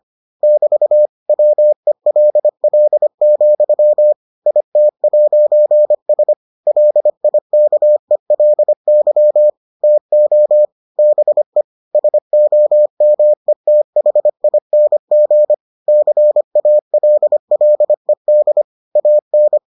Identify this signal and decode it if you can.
Morse code, heard as a keyed tone, about 25 wpm, 600 Hz. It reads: <BT> WELL, IT'S LIKELY TO BE SOMETHING CALLED AN